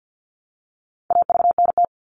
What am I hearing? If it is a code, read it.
Morse code, heard as a keyed tone, about 50 wpm, 700 Hz. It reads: AVNT
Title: All Cut Numbers - 4-Digit 50wpm